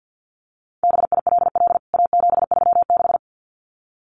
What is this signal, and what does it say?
Morse code, heard as a keyed tone, about 50 wpm, 700 Hz. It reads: BILL A736
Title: CWT Exchange 50wpm